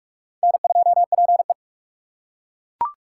Morse code, heard as a keyed tone, about 42 wpm, 700 Hz. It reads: N2PE